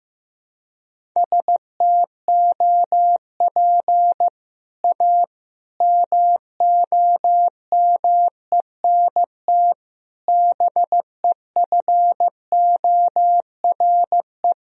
Morse code, heard as a keyed tone, about 16 wpm, 700 Hz. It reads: STOP A MOMENT BEFORE
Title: Sentences from Top 1000 Words - Review Part 1 - 15wpm